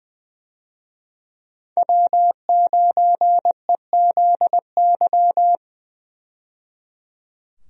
Morse code, heard as a keyed tone, about 20 wpm, 700 Hz. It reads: W9EZY
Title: USA Call Signs - 5 Characters - 20wpm